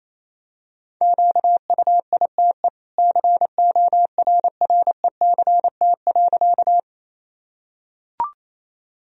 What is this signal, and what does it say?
Morse code, heard as a keyed tone, about 27 wpm, 700 Hz. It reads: QUITE CORRECT.
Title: Sentences from Top 1000 Words - Review Part 2 - 28wpm